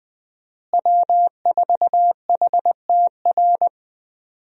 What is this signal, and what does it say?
Morse code, heard as a keyed tone, about 20 wpm, 700 Hz. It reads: W4HTR
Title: USA Call Signs - 5 Characters - Repeated 3x - 20wpm